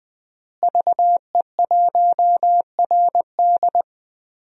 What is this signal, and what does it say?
Morse code, heard as a keyed tone, about 20 wpm, 700 Hz. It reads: VE1RD
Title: DXCC-100 Call Signs - 5 Characters - Repeated 3x - 20wpm